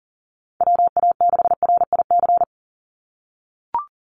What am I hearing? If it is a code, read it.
Morse code, heard as a keyed tone, about 40 wpm, 700 Hz. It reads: WA6RIC
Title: NA SOTA Chaser Callsigns 40wpm